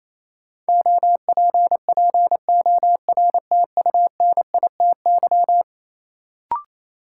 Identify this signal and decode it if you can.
Morse code, heard as a keyed tone, about 28 wpm, 700 Hz. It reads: OPPORTUNITY
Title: Sentences from Top 1000 Words - New Word Drill - 28wpm